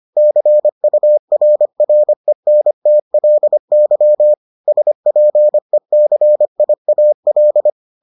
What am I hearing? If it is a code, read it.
Morse code, heard as a keyed tone, about 25 wpm, 600 Hz. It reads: CURRENTLY SPECIAL